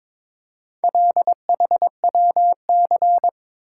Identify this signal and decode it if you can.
Morse code, heard as a keyed tone, about 22 wpm, 700 Hz. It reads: LHWC
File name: practice / 22wpm / Four Letter-Number Combinations - 22wpm